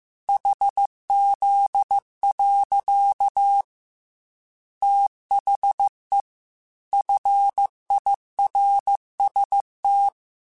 Morse code, heard as a keyed tone, about 15 wpm, 800 Hz. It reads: HZ. THE FIRST
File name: Technical Subjects / Technical Subjects-mp3s / Sw Pwr Sup @13